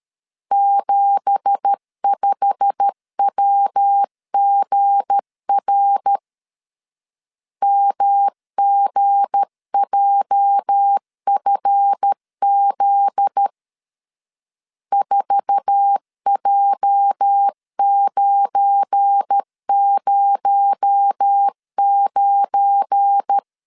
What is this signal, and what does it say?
Morse code, heard as a keyed tone, about 13 wpm, 800 Hz. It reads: 75WGR MGJFZ 4J909
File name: CW / DET / 1 9 1500C S C 12wpm 3811